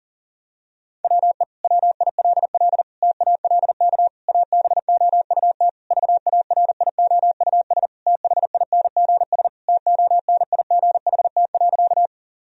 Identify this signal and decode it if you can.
Morse code, heard as a keyed tone, about 40 wpm, 700 Hz. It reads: WE WILL TALK ABOUT VARIOUS THINGS TONIGHT.
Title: Sentences from Top 1000 Words - Review Part 4 - 40wpm